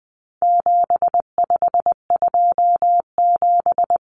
Morse code, heard as a keyed tone, about 20 wpm, 700 Hz. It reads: 7527
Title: Numbers - 4-Digit 20wpm